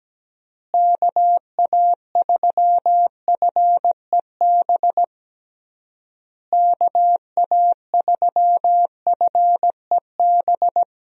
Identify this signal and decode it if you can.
Morse code, heard as a keyed tone, about 17 wpm, 700 Hz. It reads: KA3FEB KA3FEB
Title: USA Call Signs - 6 Characters - Repeated 3x - 17wpm